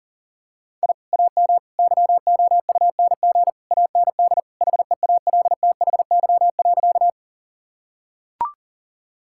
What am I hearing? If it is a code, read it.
Morse code, heard as a keyed tone, about 40 wpm, 700 Hz. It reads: I AM YOUNG AND HEALTHY.
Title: Sentences from Top 1000 Words - Review Part 5 - 40wpm